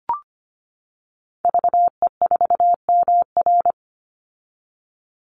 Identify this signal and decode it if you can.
Morse code, heard as a keyed tone, about 25 wpm, 700 Hz. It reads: VE4MR
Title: Call Signs - 25wpm